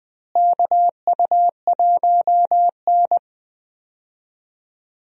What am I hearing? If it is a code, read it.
Morse code, heard as a keyed tone, about 21 wpm, 700 Hz. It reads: KU1N
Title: USA Call Signs - 4 Characters - Repeated 2x - 20wpm